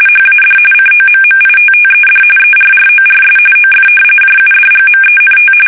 RTTY, about 45.45 baud. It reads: HAD COMMITTED ERRORS OF JUDGMENT
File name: fsk850 / battleshipnj